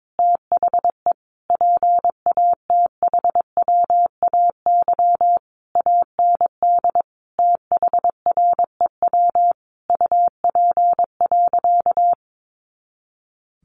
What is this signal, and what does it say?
Morse code, heard as a keyed tone, about 22 wpm, 700 Hz. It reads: THE PATHWAY AND THREW UP.
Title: A Sound Of Thunder 22wpm